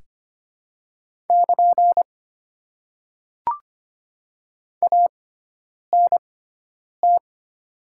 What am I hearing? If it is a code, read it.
Morse code, heard as a keyed tone, about 25 wpm, spaced as for 9 wpm, 700 Hz. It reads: <KN> ANT
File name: Top 90 QSO Elements - 6x Character Spacing - 25wpm